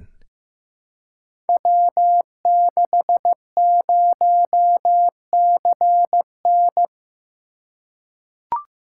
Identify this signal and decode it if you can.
Morse code, heard as a keyed tone, about 15 wpm, 700 Hz. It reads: W60CN
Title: Code Groups - 5 Letter:Numbers - 15wpm